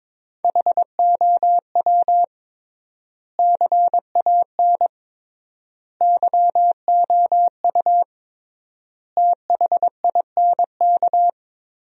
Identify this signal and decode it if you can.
Morse code, heard as a keyed tone, about 22 wpm, 700 Hz. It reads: HOW CAN YOU THINK
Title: 3x Word-Spacing - Sentences from Top 200 Words - 22wpm